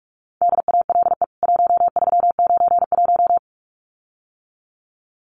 Morse code, heard as a keyed tone, about 44 wpm, 700 Hz. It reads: DALE 1391
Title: CWT Exchange 45wpm